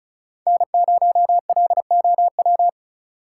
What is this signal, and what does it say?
Morse code, heard as a keyed tone, about 36 wpm, 700 Hz. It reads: N0LOW
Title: USA Call Signs - 6x Character Spacing - 35wpm